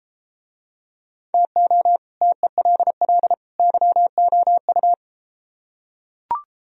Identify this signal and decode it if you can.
Morse code, heard as a keyed tone, about 29 wpm, 700 Hz. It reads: TO TELL YOU